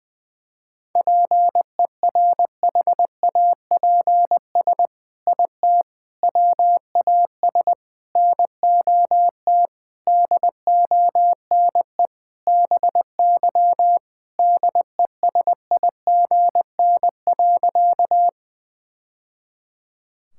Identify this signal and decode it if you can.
Morse code, heard as a keyed tone, about 20 wpm, 700 Hz. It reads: PERHAPS IT WAS NOT DONE BY DESIGN.
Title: Sentences from Top 1000 Words - Review Part 5 - 20wpm